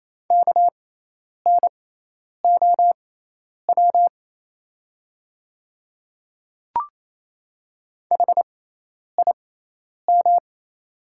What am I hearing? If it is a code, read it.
Morse code, heard as a keyed tone, about 28 wpm, spaced as for 10 wpm, 700 Hz. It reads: KNOW HIM
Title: 6x Character Spacing - Top 300 Words - 28wpm